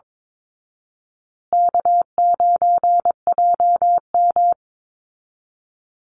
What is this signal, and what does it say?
Morse code, heard as a keyed tone, about 22 wpm, 690 Hz. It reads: K9JM
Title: Easy Call Signs Repeated 2x - 22wpm